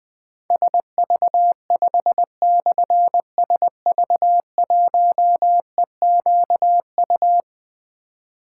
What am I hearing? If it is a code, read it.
Morse code, heard as a keyed tone, about 21 wpm, 700 Hz. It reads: SV5/SV1EQU
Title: Difficult Call Signs 20wpm